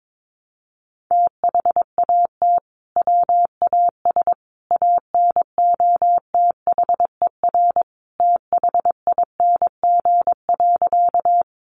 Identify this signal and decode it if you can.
Morse code, heard as a keyed tone, about 22 wpm, 700 Hz. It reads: THAT WAS ANOTHER THING.
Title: Sentences from Top 600 Words 22wpm